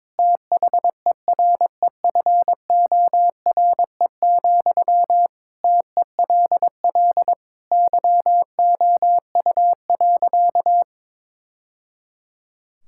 Morse code, heard as a keyed tone, about 22 wpm, 700 Hz. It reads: THEREFORE, TELL YOU.
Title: Sentences from Top 1000 Words - Review Part 5 - 22wpm